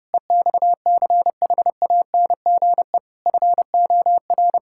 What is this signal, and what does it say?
Morse code, heard as a keyed tone, about 30 wpm, 700 Hz. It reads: EXCHANGE FOR
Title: Sentences from Top 1000 Words - Review Part 6 - 30wpm